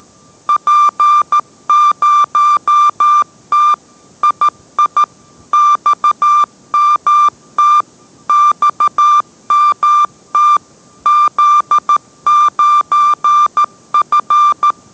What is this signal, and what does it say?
Morse code, heard as a keyed tone, about 13 wpm, 1180 Hz. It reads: P0T II XMT XMT Z9F